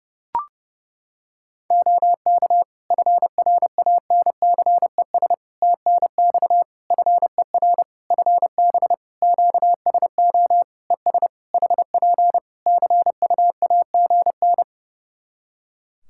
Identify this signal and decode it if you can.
Morse code, heard as a keyed tone, about 30 wpm, 700 Hz. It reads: OK FRANCES TNX FER FB QSO ES HP CUAGN
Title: Practice QSO - Exchange 4 - Ending - 30wpm